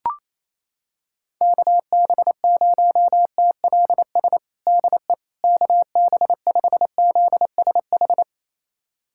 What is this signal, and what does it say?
Morse code, heard as a keyed tone, about 28 wpm, 700 Hz. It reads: KB0TLS DE KB5ZSH
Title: USA Callsign DE Callsign - 6 Characters - 28wpm